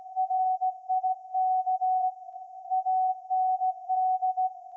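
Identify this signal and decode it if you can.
Morse code, heard as a keyed tone, about 16 wpm, 750 Hz. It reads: RIK AND
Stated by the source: Morse code from SAQ, received at a station in Putignano, Apulia, Italy, on VLF.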